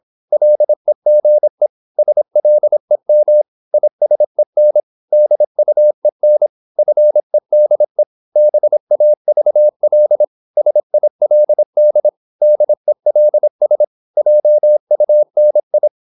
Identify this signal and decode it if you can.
Morse code, heard as a keyed tone, about 26 wpm, 600 Hz. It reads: LEGE SLEM ISEN DUEN FEDE BAVL SILD DELS JUNI
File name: Korte ord DK 26wpm